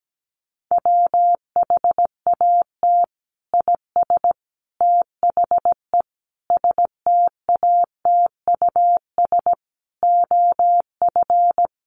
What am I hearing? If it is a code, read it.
Morse code, heard as a keyed tone, about 17 wpm, 700 Hz. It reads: WHAT IS THE STATUS OF
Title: Sentences from Top 1400 Words 17wpm